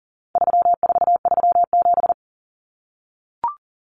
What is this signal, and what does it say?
Morse code, heard as a keyed tone, about 40 wpm, 700 Hz. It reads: 3437
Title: Common Cut Numbers - 4-Digit 40wpm